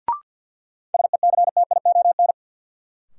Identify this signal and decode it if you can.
Morse code, heard as a keyed tone, about 50 wpm, 700 Hz. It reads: SECTION
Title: Sentences from Top 1000 Words - Review Part 5 - New Word Drill - 50wpm